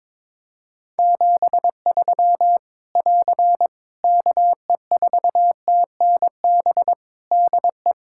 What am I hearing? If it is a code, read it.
Morse code, heard as a keyed tone, about 22 wpm, 700 Hz. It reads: 73 <AR> KE4TNB DE
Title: K1USN SST QSO Practice 22wpm